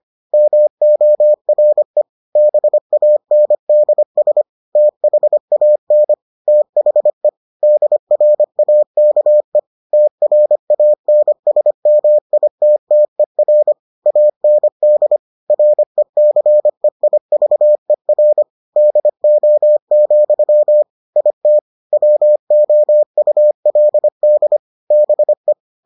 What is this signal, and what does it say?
Morse code, heard as a keyed tone, about 25 wpm, 600 Hz. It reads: MORE BANDS THAN THE DRAKE TRANSMITTER AND RECEIVER DO, IT WOULD BE